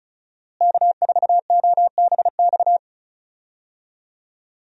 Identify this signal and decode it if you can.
Morse code, heard as a keyed tone, about 36 wpm, 700 Hz. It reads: K4OBX
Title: USA Call Signs - 5 Characters - Repeated 2x - 35wpm